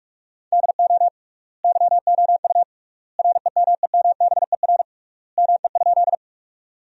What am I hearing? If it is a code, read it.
Morse code, heard as a keyed tone, about 47 wpm, 700 Hz. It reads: DO YOU REMEMBER ME?
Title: 3x Word-Spacing - Sentences from Top 400 Words - 45wpm